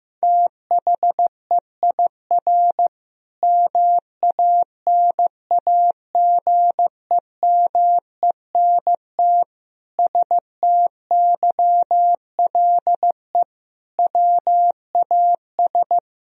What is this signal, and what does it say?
Morse code, heard as a keyed tone, about 16 wpm, 700 Hz. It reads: THEIR MANAGEMENT STYLE WAS
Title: Sentences from Top 1000 Words - Review Part 4 - 15wpm